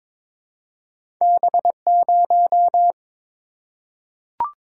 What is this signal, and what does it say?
Morse code, heard as a keyed tone, about 22 wpm, 700 Hz. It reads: B0
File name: practice / 22wpm / Two Letter-Number Combinations - 22wpm